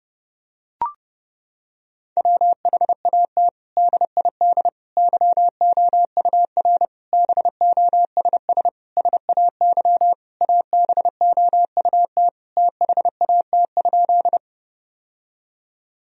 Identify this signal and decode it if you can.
Morse code, heard as a keyed tone, about 30 wpm, 700 Hz. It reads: WHAT DID YOUR BOSS SAY ABOUT THAT?
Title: Sentences from Top 1000 Words - Review Part 4 - 30wpm